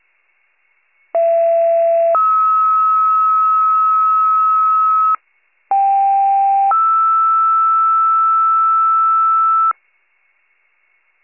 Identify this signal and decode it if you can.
Two-tone page, heard as a one-second tone, then a three-second tone, about 670 then 1290 Hz, about 790 then 1320 Hz. It is heard from 1.1 to 9.7 s.